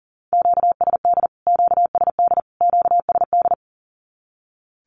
Morse code, heard as a keyed tone, about 40 wpm, 700 Hz. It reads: QSD QSD QSD
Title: Q-Codes Repeated 3x 40wpm